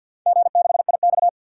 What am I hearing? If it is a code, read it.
Morse code, heard as a keyed tone, about 47 wpm, 700 Hz. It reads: G6IX
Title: DXCC-100 Call Signs - 4 Characters - 50wpm